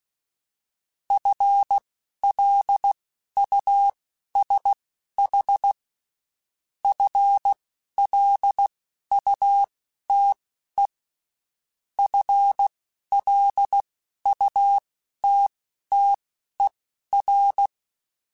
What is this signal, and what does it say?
Morse code, heard as a keyed tone, about 16 wpm, spaced as for 11 wpm, 800 Hz. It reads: FLUSH FLUTE FLUTTER
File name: Spaced / FL Runs sp mp3s / FL Runs sp @15